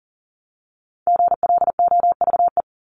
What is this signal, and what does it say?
Morse code, heard as a keyed tone, about 40 wpm, 700 Hz. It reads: GLOVE
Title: Top 1751-2000 Words 40wpm